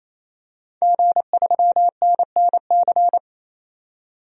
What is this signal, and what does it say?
Morse code, heard as a keyed tone, about 28 wpm, 700 Hz. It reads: G3NNC
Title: DXCC-100 Call Signs - 5 Characters - 28wpm